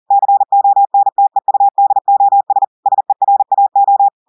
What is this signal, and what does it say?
Morse code, heard as a keyed tone, about 40 wpm, 800 Hz. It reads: CONTEUDOS SERAO
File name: audios / portugais / portugais texte en clair 2 vitesse 40 mots minutes